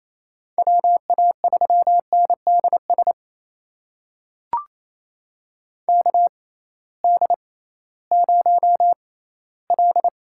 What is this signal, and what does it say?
Morse code, heard as a keyed tone, about 27 wpm, spaced as for 16 wpm, 700 Hz. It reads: WA3NDS KD0L
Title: USA Call Signs - 6 Characters - 6x Character Spacing - 28wpm